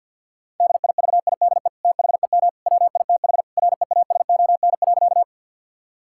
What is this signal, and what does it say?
Morse code, heard as a keyed tone, about 50 wpm, 700 Hz. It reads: DIVIDE THEM WITH REASON.
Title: Sentences from Top 1600 Words 50wpm